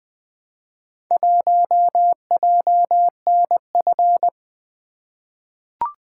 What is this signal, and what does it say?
Morse code, heard as a keyed tone, about 20 wpm, 700 Hz. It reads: 1JNF
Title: Four Letter-Number Combinations - 20wpm